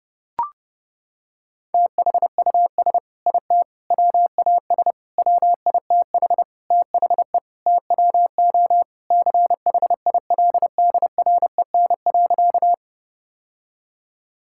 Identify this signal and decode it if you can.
Morse code, heard as a keyed tone, about 30 wpm, 700 Hz. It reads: THUS IT WAS WITH THE TWO CHILDREN.
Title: Sentences from Top 1000 Words - Review Part 5 - 30wpm